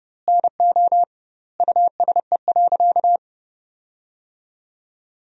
Morse code, heard as a keyed tone, about 30 wpm, 700 Hz. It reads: NO USE.
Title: 2x Word-Spacing - Sentences from Top 100 Words - 30wpm